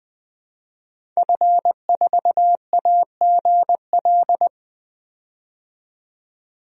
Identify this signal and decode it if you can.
Morse code, heard as a keyed tone, about 20 wpm, 700 Hz. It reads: F4AGL
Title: DXCC-100 Call Signs - 5 Characters - 20wpm